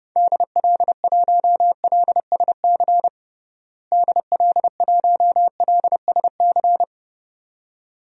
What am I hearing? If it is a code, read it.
Morse code, heard as a keyed tone, about 30 wpm, 700 Hz. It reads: DL1LSC DL1LSC
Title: DXCC-100 Call Signs - 6 Characters - Repeated 2x - 30wpm